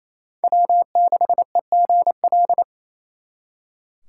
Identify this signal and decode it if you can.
Morse code, heard as a keyed tone, about 36 wpm, 700 Hz. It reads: W6EGL